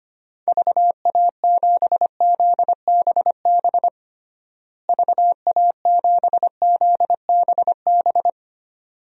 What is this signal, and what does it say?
Morse code, heard as a keyed tone, about 25 wpm, 700 Hz. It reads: VA7ZBB VA7ZBB
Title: DXCC-100 Call Signs - 6 Characters - Repeated 2x - 25wpm